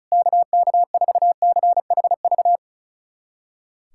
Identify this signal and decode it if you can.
Morse code, heard as a keyed tone, about 36 wpm, 700 Hz. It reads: KK4CHV
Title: USA Callsign DE Callsign 35wpm